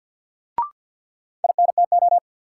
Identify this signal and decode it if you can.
Morse code, heard as a keyed tone, about 48 wpm, 700 Hz. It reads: INTO
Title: Sets of 2 Words - New Word Drill - 50wpm